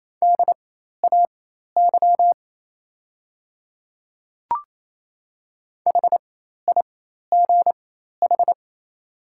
Morse code, heard as a keyed tone, about 27 wpm, spaced as for 13 wpm, 700 Hz. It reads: DAY HIGH
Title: 4x Character Spacing - Top 300 Words - 28wpm